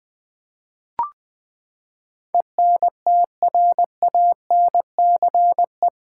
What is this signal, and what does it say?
Morse code, heard as a keyed tone, about 20 wpm, 700 Hz. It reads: ENTRANCE
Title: Eight Letter Words - 20wpm